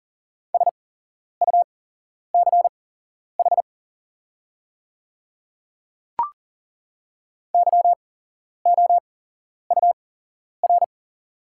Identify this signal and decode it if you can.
Morse code, heard as a keyed tone, about 40 wpm, spaced as for 11 wpm, 690 Hz. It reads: SUCH YOUR
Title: 8x Character Spacing - Top 200 Words - 40wpm